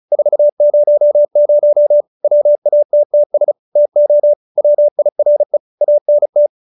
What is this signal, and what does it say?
Morse code, heard as a keyed tone, about 34 wpm, 590 Hz. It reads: 400 WATTS TO WIRE ANT